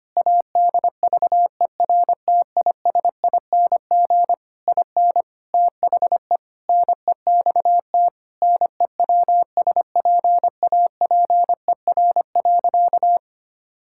Morse code, heard as a keyed tone, about 25 wpm, 700 Hz. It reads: ADVERTISING IN THE NEXT NEWSPAPER.
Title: Sentences from Top 1000 Words - Review Part 6 - 25wpm